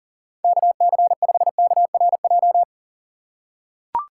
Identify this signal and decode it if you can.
Morse code, heard as a keyed tone, about 40 wpm, 700 Hz. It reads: KC5KRJ
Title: USA Call Signs - 6 Characters - 4x Character Spacing - 40wpm